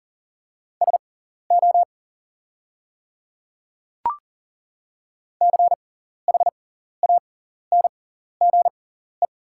Morse code, heard as a keyed tone, about 40 wpm, spaced as for 14 wpm, 690 Hz. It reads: SO CHANGE
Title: 6x Character Spacing - Top 200 Words - 40wpm